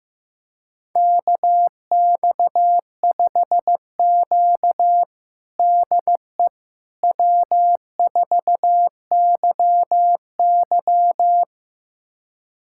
Morse code, heard as a keyed tone, about 15 wpm, 700 Hz. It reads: KX5Q DE W4YY
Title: USA Callsign DE Callsign - 4 Characters - 15wpm